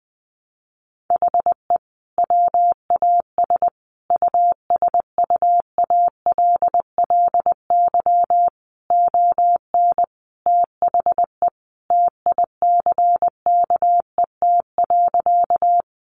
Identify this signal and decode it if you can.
Morse code, heard as a keyed tone, about 20 wpm, 700 Hz. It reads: HE WAS USUALLY ON THE TICKET.
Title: Sentences from Top 1400 Words 20wpm